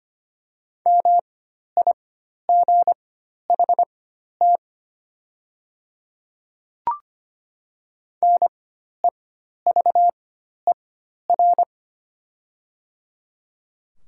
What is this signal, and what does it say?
Morse code, heard as a keyed tone, about 25 wpm, spaced as for 12 wpm, 700 Hz. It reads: MIGHT NEVER
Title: 4x Character Spacing - Top 300 Words - 25wpm